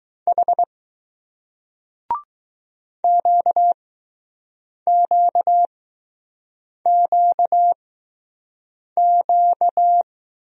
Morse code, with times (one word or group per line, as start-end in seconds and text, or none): 0.3-0.6 H
3.0-10.0 QQQQ